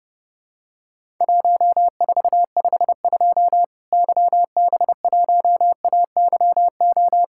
Morse code, heard as a keyed tone, about 30 wpm, 700 Hz. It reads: 1452 YB1AYO
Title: Worked All Europe DX Contest - QTC Exchange - 30wpm